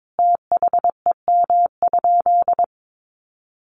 Morse code, heard as a keyed tone, about 22 wpm, 700 Hz. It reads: THEM?
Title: Sentences from Top 700 Words 22wpm